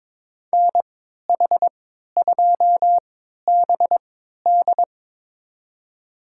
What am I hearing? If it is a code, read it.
Morse code, heard as a keyed tone, about 22 wpm, spaced as for 12 wpm, 700 Hz. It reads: NH2BD
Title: USA Call Signs - 5 Characters - 3x Character Spacing - 22wpm